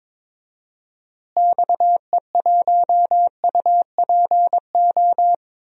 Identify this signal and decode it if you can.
Morse code, heard as a keyed tone, about 22 wpm, 700 Hz. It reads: XE1UPO
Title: DXCC-100 Call Signs - 6 Characters - Repeated 3x - 22wpm